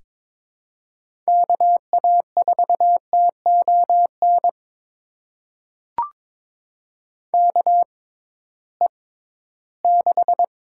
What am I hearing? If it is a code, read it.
Morse code, heard as a keyed tone, about 22 wpm, spaced as for 14 wpm, 700 Hz. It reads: KA4TON KE6
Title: USA Call Signs - 6 Characters - 6x Character Spacing - 22wpm